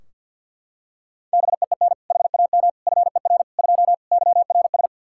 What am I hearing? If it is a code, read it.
Morse code, heard as a keyed tone, about 53 wpm, 700 Hz. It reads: BEEN HAM FER 2 YRS
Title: K1USN SST QSO Practice 50wpm